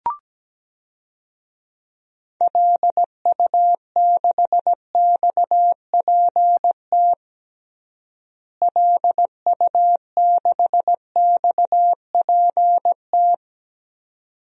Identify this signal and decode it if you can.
Morse code, heard as a keyed tone, about 17 wpm, 700 Hz. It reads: LU6XPT LU6XPT
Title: DXCC-100 Call Signs - 6 Characters - Repeated 2x - 17wpm